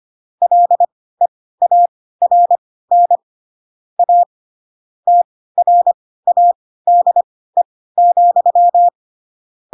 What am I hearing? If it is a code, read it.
Morse code, heard as a keyed tone, about 25 wpm, spaced as for 16 wpm, 700 Hz. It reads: LEARN A TRADE,
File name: Pinocchio / Pinocchio ch4 10 25 ik1hge